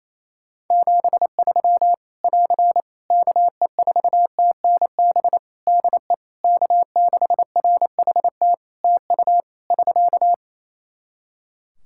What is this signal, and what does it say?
Morse code, heard as a keyed tone, about 28 wpm, 700 Hz. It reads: 73 <AR> KE4TNB DE K6RHT TU <SK>
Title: Practice QSO - Exchange 4 - Ending - 28wpm